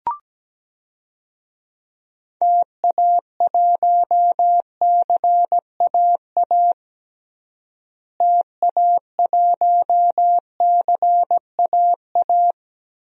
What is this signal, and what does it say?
Morse code, heard as a keyed tone, about 17 wpm, 700 Hz. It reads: TA1CAA TA1CAA
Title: DXCC-100 Call Signs - 6 Characters - Repeated 2x - 17wpm